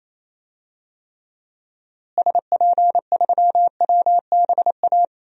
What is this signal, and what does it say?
Morse code, heard as a keyed tone, about 28 wpm, 700 Hz. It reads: SP3WBA
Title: DXCC-100 Call Signs - 6 Characters - Repeated 2x - 28wpm